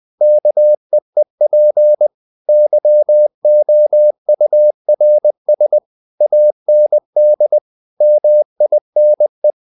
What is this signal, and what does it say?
Morse code, heard as a keyed tone, about 20 wpm, 600 Hz. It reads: KEEP YOURS AND MINE
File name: the old man and the sea / the old man and the sea part 1 20wpm